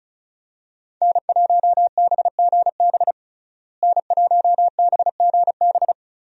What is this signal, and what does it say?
Morse code, heard as a keyed tone, about 34 wpm, 700 Hz. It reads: N1BGB N1BGB
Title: USA Call Signs - 5 Characters - Repeated 2x - 35wpm